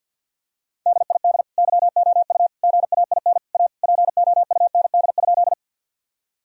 Morse code, heard as a keyed tone, about 50 wpm, 700 Hz. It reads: DID YOU GAIN A POUND?
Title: Sentences from Top 1000 Words - Review Part 2 - 50wpm